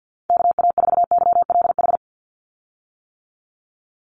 Morse code, heard as a keyed tone, about 48 wpm, 700 Hz. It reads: KA4YLH
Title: USA Call Signs 50wpm